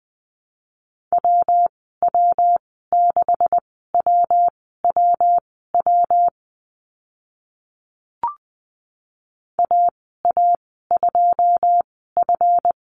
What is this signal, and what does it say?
Morse code, heard as a keyed tone, about 20 wpm, spaced as for 14 wpm, 700 Hz. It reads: WW6WWW AA2F
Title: 140 Lesson 23g - Letter f - Call Signs 20wpm